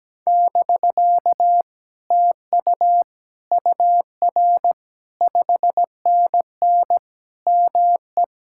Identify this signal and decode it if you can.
Morse code, heard as a keyed tone, about 17 wpm, 700 Hz. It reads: <BK> TU UR 5NN ME
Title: Speed-Racing Common QSO Phrases - 17wpm